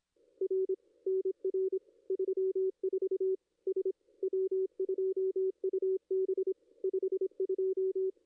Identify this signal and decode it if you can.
Morse code, heard as a keyed tone, about 26 wpm, 440 Hz. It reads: R NR 34 S W2UB 52
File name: contests / ss-cw-k5tr-op-n5rz -2300